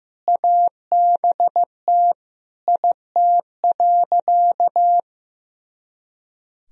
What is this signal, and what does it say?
Morse code, heard as a keyed tone, about 15 wpm, 700 Hz. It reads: ABT IT.
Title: Sentences from Top 500 Words and Abbreviations 15wpm